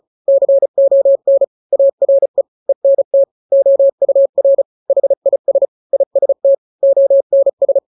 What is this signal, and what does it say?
Morse code, heard as a keyed tone, about 34 wpm, 550 Hz. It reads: CON ARE ENT OUR HIS IST ONS
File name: 3-grams 35 30 WPM